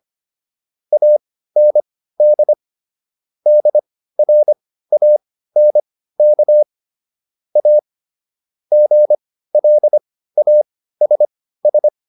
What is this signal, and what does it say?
Morse code, heard as a keyed tone, about 25 wpm, spaced as for 15 wpm, 600 Hz. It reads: AND DRANK A GLASS